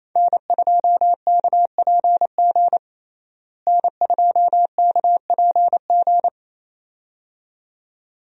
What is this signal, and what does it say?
Morse code, heard as a keyed tone, about 28 wpm, 700 Hz. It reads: N2KPG N2KPG
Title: USA Call Signs - 5 Characters - Repeated 2x - 28wpm